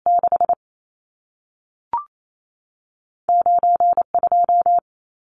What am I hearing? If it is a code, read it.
Morse code, heard as a keyed tone, about 28 wpm, spaced as for 3 wpm, 700 Hz. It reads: 69 2
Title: Numbers - 2-Digit 28wpm